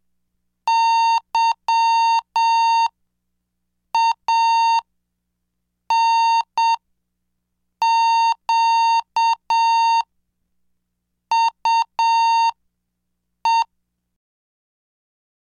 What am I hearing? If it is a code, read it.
Morse code, heard as a keyed tone, about 7 wpm, spaced as for 5 wpm, 920 Hz. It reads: YANQUE
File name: cursos / CDestudo / 20 Track 20